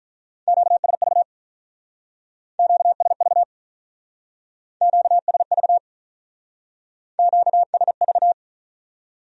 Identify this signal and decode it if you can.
Morse code, heard as a keyed tone, about 45 wpm, 700 Hz. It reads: QSV QSV QSV QSV